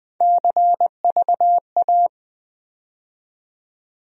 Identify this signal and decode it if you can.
Morse code, heard as a keyed tone, about 20 wpm, 700 Hz. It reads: CVA
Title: USA Call Sign Postfixes - 3 Letters 20wpm